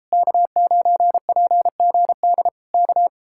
Morse code, heard as a keyed tone, about 33 wpm, 700 Hz. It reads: K9PGD K